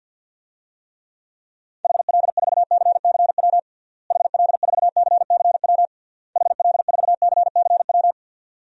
Morse code, heard as a keyed tone, about 51 wpm, 700 Hz. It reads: HL4CCW HL4CCW HL4CCW
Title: DXCC-100 Call Signs - 6 Characters - Repeated 3x - 50wpm